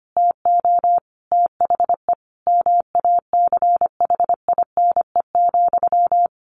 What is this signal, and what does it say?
Morse code, heard as a keyed tone, about 25 wpm, 700 Hz. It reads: TO THE MACHINE,
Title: A Sound Of Thunder 25wpm